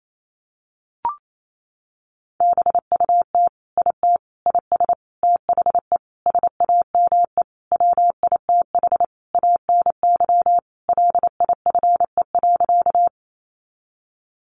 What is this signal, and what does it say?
Morse code, heard as a keyed tone, about 28 wpm, 700 Hz. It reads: BUT IT IS THE SAME WITH ANY LIFE.
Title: Sentences from Top 300 Words 28wpm